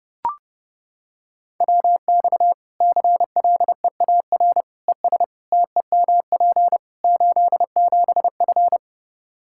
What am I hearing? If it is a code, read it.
Morse code, heard as a keyed tone, about 30 wpm, 700 Hz. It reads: WX CLEAR ES TEMP 87F
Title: Practice QSO - Exchange 2 - Info - 30wpm